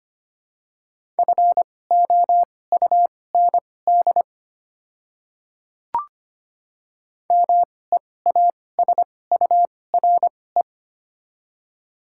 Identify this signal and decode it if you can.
Morse code, heard as a keyed tone, about 25 wpm, spaced as for 18 wpm, 700 Hz. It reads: FOUND MEASURE
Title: 2x Character Spacing - Top 500 Words - 25wpm